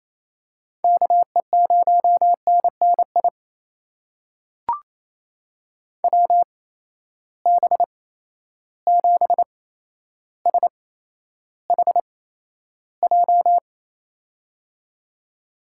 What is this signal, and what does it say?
Morse code, heard as a keyed tone, about 27 wpm, spaced as for 12 wpm, 700 Hz. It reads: KE0NNI WB7SHJ
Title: USA Call Signs - 8x Character Spacing - 28wpm